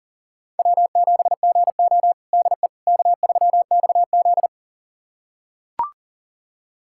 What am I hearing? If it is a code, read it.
Morse code, heard as a keyed tone, about 40 wpm, 700 Hz. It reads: W7GO DE K3XZ
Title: USA Callsign DE Callsign - 4 Characters - 40wpm